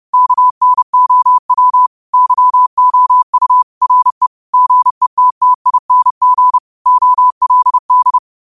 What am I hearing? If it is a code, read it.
Morse code, heard as a keyed tone, about 30 wpm, 1000 Hz. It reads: KNOW YOU RE GETTING OLD